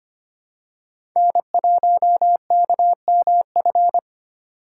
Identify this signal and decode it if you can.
Morse code, heard as a keyed tone, about 25 wpm, 700 Hz. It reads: N1KMF
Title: USA Call Signs - 5 Characters - 3x Character Spacing - 25wpm